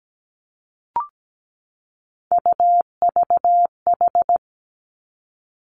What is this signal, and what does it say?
Morse code, heard as a keyed tone, about 17 wpm, 700 Hz. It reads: UVH
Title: Three Challenging Letter-Number Combinations 17wpm